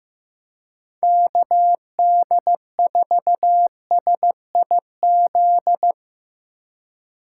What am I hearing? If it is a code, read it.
Morse code, heard as a keyed tone, about 17 wpm, 700 Hz. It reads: KD4SIZ
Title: Speed Racing - USA 2x3 Call Signs 15wpm